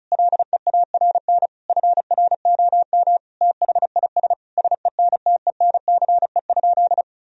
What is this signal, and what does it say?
Morse code, heard as a keyed tone, about 36 wpm, 700 Hz. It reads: LEARN FROM THIS SENTENCE?
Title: Sentences from Top 1000 Words - Review Part 1 - 35wpm